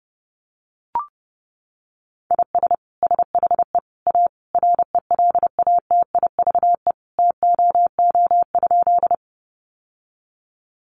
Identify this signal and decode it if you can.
Morse code, heard as a keyed tone, about 30 wpm, 700 Hz. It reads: IS SHE A RELATIVE TOO?
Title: Sentences from Top 1400 Words 30wpm